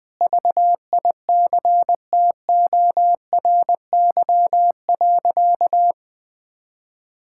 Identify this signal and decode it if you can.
Morse code, heard as a keyed tone, about 21 wpm, 700 Hz. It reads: VICTORY.
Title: Sentences with 9-Word Phrases - 20wpm